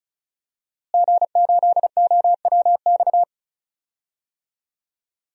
Morse code, heard as a keyed tone, about 34 wpm, 700 Hz. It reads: G8OWX
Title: DXCC-100 Call Signs - 5 Characters - Repeated 3x - 35wpm